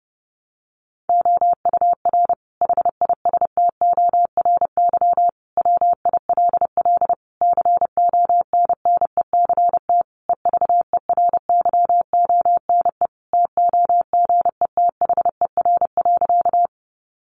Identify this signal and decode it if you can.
Morse code, heard as a keyed tone, about 28 wpm, 700 Hz. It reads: OUR HISTORY WILL CONNECT EVERYONE TOGETHER.
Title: Sentences from Top 1500 Words 30wpm